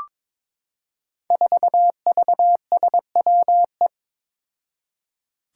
Morse code, heard as a keyed tone, about 22 wpm, 700 Hz. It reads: 4VSWE
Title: Five Challenging Letter-Number Combinations 22wpm